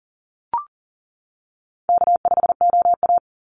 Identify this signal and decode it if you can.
Morse code, heard as a keyed tone, about 40 wpm, 700 Hz. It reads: K5OA
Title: Call Signs - 40wpm